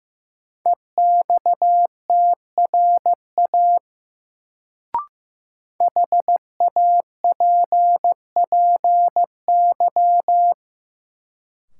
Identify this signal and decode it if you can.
Morse code, heard as a keyed tone, about 15 wpm, 700 Hz. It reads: EXTRA HAPPY
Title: Sentences from Top 1000 Words - New Word Drill - 15wpm